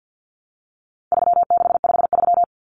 Morse code, heard as a keyed tone, about 50 wpm, 700 Hz. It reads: 3653
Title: Common Cut Numbers - 4-Digit 50wpm